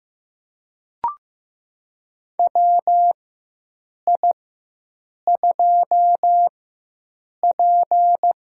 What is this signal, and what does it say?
Morse code, heard as a keyed tone, about 15 wpm, spaced as for 7 wpm, 700 Hz. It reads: WI2P
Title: USA Call Signs - 4 Characters - 4x Character Spacing - 15wpm